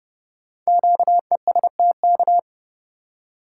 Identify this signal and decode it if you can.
Morse code, heard as a keyed tone, about 30 wpm, 700 Hz. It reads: QESTK
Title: Code Groups - 5 Letters - 30wpm